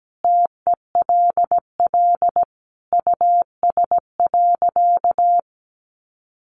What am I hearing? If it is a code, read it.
Morse code, heard as a keyed tone, about 17 wpm, 700 Hz. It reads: TELL US.
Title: Sentences from Top 400 Words 17wpm